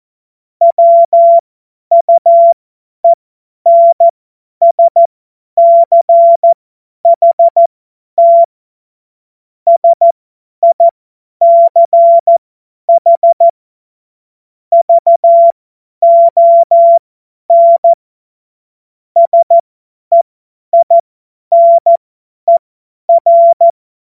Morse code, heard as a keyed tone, about 14 wpm, spaced as for 10 wpm, 680 Hz. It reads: WUENSCHT SICH VON SEINER
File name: Morsekurs / Witze / Witz 10. Januar 2023 50 F70